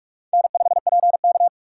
Morse code, heard as a keyed tone, about 44 wpm, 700 Hz. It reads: N5PK